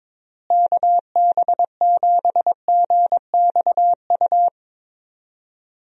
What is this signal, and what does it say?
Morse code, heard as a keyed tone, about 22 wpm, 700 Hz. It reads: KB7GXU
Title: USA Call Signs - 6 Characters - Repeated 3x - 22wpm